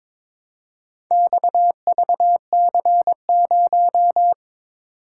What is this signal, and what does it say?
Morse code, heard as a keyed tone, about 22 wpm, 700 Hz. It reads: XVC0
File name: practice / 22wpm / Four Letter-Number Combinations - 22wpm